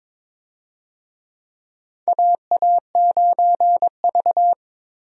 Morse code, heard as a keyed tone, about 22 wpm, 700 Hz. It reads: AA9V
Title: USA Call Signs - 4 Characters - 22wpm